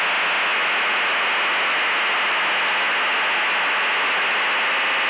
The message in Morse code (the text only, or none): none